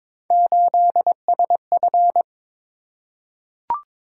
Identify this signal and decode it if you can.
Morse code, heard as a keyed tone, about 22 wpm, 700 Hz. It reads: 8SF
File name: practice / 22wpm / Three Letter-Number Combinations - 22wpm